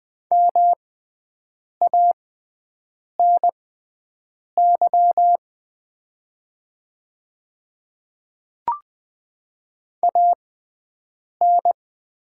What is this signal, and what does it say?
Morse code, heard as a keyed tone, about 20 wpm, spaced as for 7 wpm, 690 Hz. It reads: MANY AN
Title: 6x Character Spacing - Top 200 Words - 20wpm